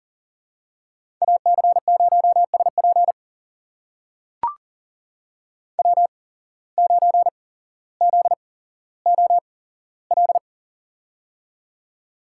Morse code, heard as a keyed tone, about 38 wpm, spaced as for 17 wpm, 700 Hz. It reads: AC0SP W9ZOL
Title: USA Call Signs - 5 Characters - 8x Character Spacing - 40wpm